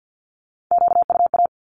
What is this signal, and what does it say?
Morse code, heard as a keyed tone, about 50 wpm, 700 Hz. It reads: QUA
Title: Q-Codes with Spoken Meaning - 50wpm